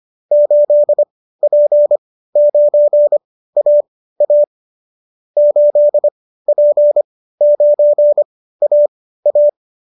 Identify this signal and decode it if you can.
Morse code, heard as a keyed tone, about 25 wpm, spaced as for 15 wpm, 590 Hz. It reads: 8P9AA 8P9AA